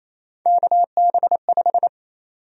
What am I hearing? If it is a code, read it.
Morse code, heard as a keyed tone, about 28 wpm, 700 Hz. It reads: KB5
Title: USA Call Sign Prefixes - Repeated 2x - 28wpm